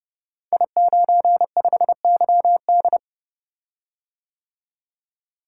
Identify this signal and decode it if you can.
Morse code, heard as a keyed tone, about 30 wpm, 700 Hz. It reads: I95YD
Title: Code Groups - 5 Letter:Numbers - 30wpm